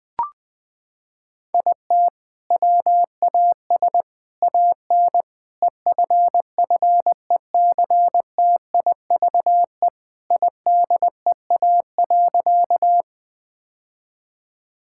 Morse code, heard as a keyed tone, about 20 wpm, 700 Hz. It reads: IT WAS AN EFFECTIVE IDEA.
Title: Sentences from Top 1000 Words - Review Part 6 - 20wpm